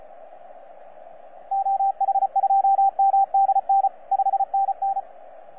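Morse code, heard as a keyed tone, about 33 wpm, 780 Hz, with radio noise. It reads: OH2MDN 5NN
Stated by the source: FS/LJ2IJ heard on 12 metres CW, from DX World.net.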